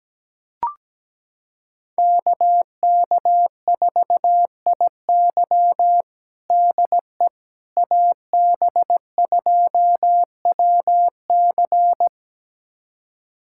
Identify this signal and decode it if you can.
Morse code, heard as a keyed tone, about 17 wpm, 700 Hz. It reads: KK4IY DE AB2WC
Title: USA Callsign DE Callsign - 5 Characters - 17wpm